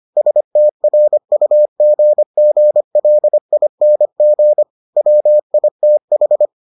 Morse code, heard as a keyed tone, about 25 wpm, 600 Hz. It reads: STRUGGLING WITH